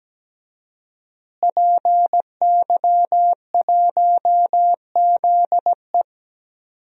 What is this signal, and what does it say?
Morse code, heard as a keyed tone, about 17 wpm, 700 Hz. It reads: PY1ZE
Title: DXCC-100 Call Signs - 5 Characters - Repeated 3x - 17wpm